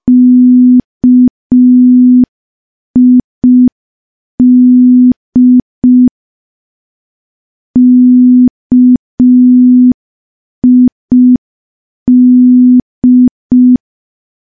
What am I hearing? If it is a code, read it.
Morse code, heard as a keyed tone, about 5 wpm, 250 Hz. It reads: KID KID
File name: morse / morse-training-05WPM